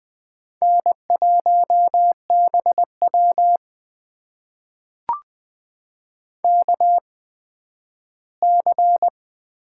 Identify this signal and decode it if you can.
Morse code, heard as a keyed tone, about 20 wpm, spaced as for 12 wpm, 700 Hz. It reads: N1BW KC
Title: USA Call Signs - 8x Character Spacing - 20wpm